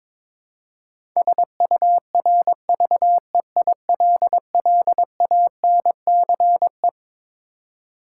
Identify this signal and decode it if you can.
Morse code, heard as a keyed tone, about 21 wpm, 700 Hz. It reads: SURVEILLANCE
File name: Twelve Letter Words - 22wpm